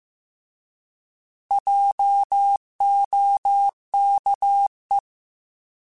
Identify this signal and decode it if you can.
Morse code, heard as a keyed tone, about 15 wpm, 800 Hz. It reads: JOKE
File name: Alphabetical / Alphabetical mp3s / Js @13